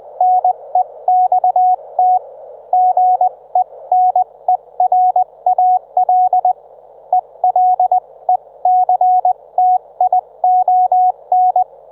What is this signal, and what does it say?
Morse code, heard as a keyed tone, about 20 wpm, spaced as for 17 wpm, 730 Hz. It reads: NEXT GENERAL ELECTION
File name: Trivia